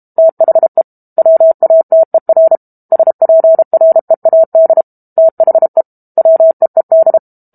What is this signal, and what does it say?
Morse code, heard as a keyed tone, about 32 wpm, 650 Hz. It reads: THE WATER SPREAD THE WEED
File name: Book Two / 06-Chapter 6 - 32 WPM